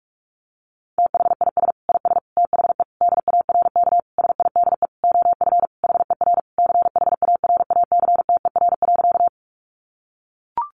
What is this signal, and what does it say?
Morse code, heard as a keyed tone, about 44 wpm, 700 Hz. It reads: THIS IS THE DARK SIDE OF HER CHARACTER.
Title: Sentences from Top 1200 Words 45wpm